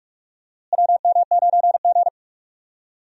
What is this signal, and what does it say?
Morse code, heard as a keyed tone, about 47 wpm, 700 Hz. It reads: WM9G
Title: USA Call Signs - 4 Characters - Repeated 3x - 45wpm